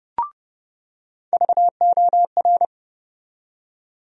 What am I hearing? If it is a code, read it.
Morse code, heard as a keyed tone, about 30 wpm, 700 Hz. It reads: VOR
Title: USA Call Sign Postfixes - 3 Letters 30wpm